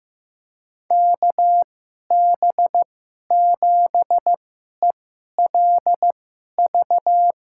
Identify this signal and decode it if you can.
Morse code, heard as a keyed tone, about 15 wpm, spaced as for 11 wpm, 700 Hz. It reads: KB7ELV
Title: USA Call Signs - 6 Characters - 2x Character Spacing - 15wpm